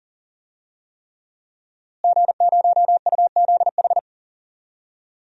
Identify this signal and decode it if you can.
Morse code, heard as a keyed tone, about 40 wpm, 700 Hz. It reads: G0UZH
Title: DXCC-100 Call Signs - 5 Characters - Repeated 3x - 40wpm